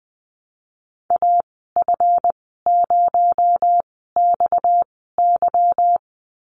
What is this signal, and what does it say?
Morse code, heard as a keyed tone, about 20 wpm, spaced as for 14 wpm, 700 Hz. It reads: AF0XY
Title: 245 Lesson 39g - Letter z - Call Signs 20wpm